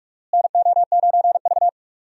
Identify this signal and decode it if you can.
Morse code, heard as a keyed tone, about 46 wpm, 700 Hz. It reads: NO9V
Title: USA Call Signs - 4 Characters - Repeated 2x - 45wpm